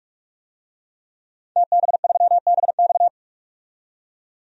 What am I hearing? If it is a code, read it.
Morse code, heard as a keyed tone, about 46 wpm, 700 Hz. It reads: TB3BX
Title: DXCC-100 Call Signs - 5 Characters - 45wpm